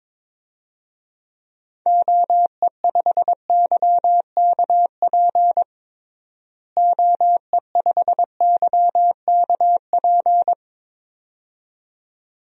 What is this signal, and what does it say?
Morse code, heard as a keyed tone, about 22 wpm, 700 Hz. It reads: OE5YKP OE5YKP
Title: DXCC-100 Call Signs - 6 Characters - Repeated 2x - 22wpm